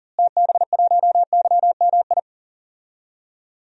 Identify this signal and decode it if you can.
Morse code, heard as a keyed tone, about 40 wpm, 700 Hz. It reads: TB1YMI
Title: DXCC-100 Call Signs - 6 Characters - Repeated 2x - 40wpm